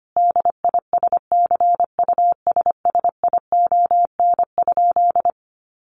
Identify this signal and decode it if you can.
Morse code, heard as a keyed tone, about 25 wpm, 700 Hz. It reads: DISCUSSION?
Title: Sentences from Top 1100 Words 25wpm